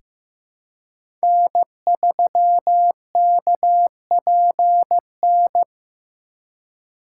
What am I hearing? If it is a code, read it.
Morse code, heard as a keyed tone, about 15 wpm, 700 Hz. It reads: N3KPN
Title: USA Call Signs - 5 Characters - 4x Character Spacing - 15wpm